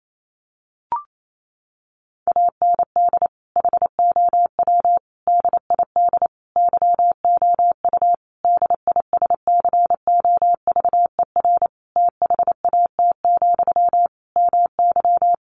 Morse code, heard as a keyed tone, about 28 wpm, 700 Hz. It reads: AND HOW DID YOU DISCOVER THAT, MY
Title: Sentences from Top 1100 Words 28wpm